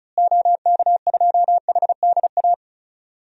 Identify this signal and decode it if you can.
Morse code, heard as a keyed tone, about 36 wpm, 700 Hz. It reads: OK2HDA
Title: DXCC-100 Call Signs - 6 Characters - Repeated 3x - 35wpm